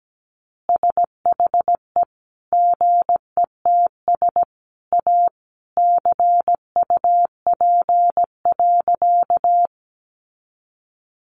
Morse code, heard as a keyed tone, about 17 wpm, 700 Hz. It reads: SHE GETS A CUP.
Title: Easy Phrases 17wpm